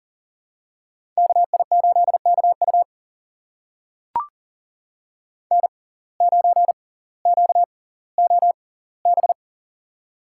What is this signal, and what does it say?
Morse code, heard as a keyed tone, about 38 wpm, spaced as for 20 wpm, 700 Hz. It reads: KI8KU N9QOB
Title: USA Call Signs - 5 Characters - 6x Character Spacing - 40wpm